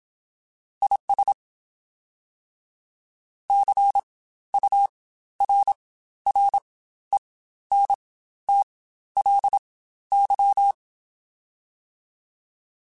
Morse code, heard as a keyed tone, about 27 wpm, spaced as for 13 wpm, 800 Hz. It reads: IS CURRENTLY
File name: Spaced Characters / Spaced Characters-mp3s / Weather 5 22 sp @25